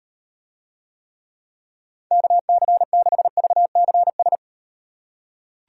Morse code, heard as a keyed tone, about 38 wpm, 700 Hz. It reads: KC6VCS